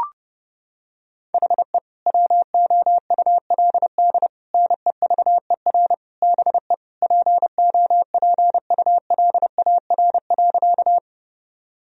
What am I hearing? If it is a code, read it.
Morse code, heard as a keyed tone, about 30 wpm, 700 Hz. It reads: HE WOULD NEVER BE POPULAR.
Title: Sentences from Top 1000 Words - Review Part 3 - 30wpm